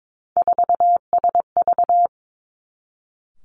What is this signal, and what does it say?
Morse code, heard as a keyed tone, about 22 wpm, 700 Hz. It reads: 4SV
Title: Three Challenging Letter-Number Combinations 22wpm